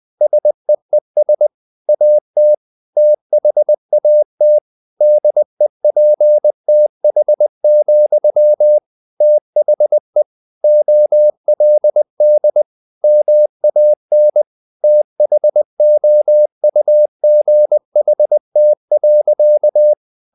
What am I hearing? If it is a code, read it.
Morse code, heard as a keyed tone, about 20 wpm, 600 Hz. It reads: SEES AT THAT DEPTH, THE OLD MAN THOUGHT.
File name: the old man and the sea / the old man and the sea part 1 20wpm